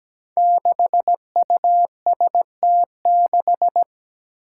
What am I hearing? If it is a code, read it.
Morse code, heard as a keyed tone, about 17 wpm, 700 Hz. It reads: 6UST6
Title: Five Challenging Letter-Number Combinations 17wpm